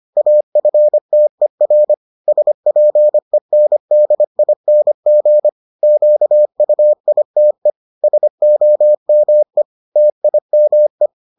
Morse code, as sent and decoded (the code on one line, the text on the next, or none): .- ..-. - . .-. / ... .--. -. -.. .. -. --. / --.- ..- .. - . / ... --- -- . / - .. -- .
AFTER SPENDING QUITE SOME TIME